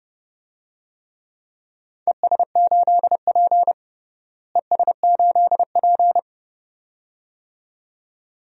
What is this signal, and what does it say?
Morse code, heard as a keyed tone, about 30 wpm, 700 Hz. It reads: ES8P ES8P
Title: DXCC-100 Call Signs - 4 Characters - Repeated 2x - 30wpm